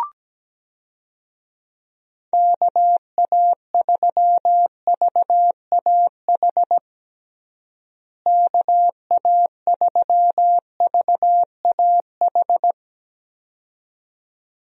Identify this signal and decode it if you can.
Morse code, heard as a keyed tone, about 17 wpm, 700 Hz. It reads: KA3VAH KA3VAH
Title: USA Call Signs - 6 Characters - Repeated 2x - 17wpm